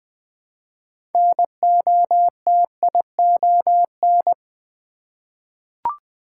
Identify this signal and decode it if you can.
Morse code, heard as a keyed tone, about 20 wpm, 700 Hz. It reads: NOTION
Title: 6 Letter Words - First 3 Letters Given - 20wpm